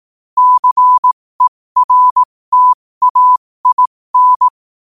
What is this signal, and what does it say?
Morse code, heard as a keyed tone, about 18 wpm, spaced as for 15 wpm, 1000 Hz. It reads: CERTAIN
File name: qotdfiles / 15wpm